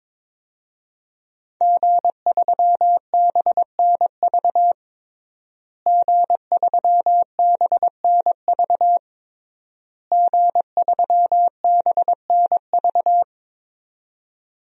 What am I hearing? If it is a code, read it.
Morse code, heard as a keyed tone, about 22 wpm, 700 Hz. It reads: G3BNV G3BNV G3BNV
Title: DXCC-100 Call Signs - 5 Characters - Repeated 3x - 22wpm